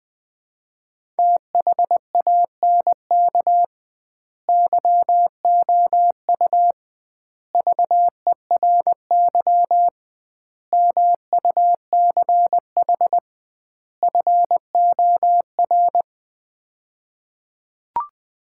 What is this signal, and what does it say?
Morse code, heard as a keyed tone, about 21 wpm, 700 Hz. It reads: THANK YOU VERY MUCH FOR
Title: 2x Word-Spacing - Sets of 5 Words - 20wpm